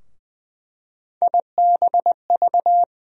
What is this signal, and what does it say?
Morse code, heard as a keyed tone, about 20 wpm, 700 Hz. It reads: IBV
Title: USA Call Sign Postfixes - 3 Letters 20wpm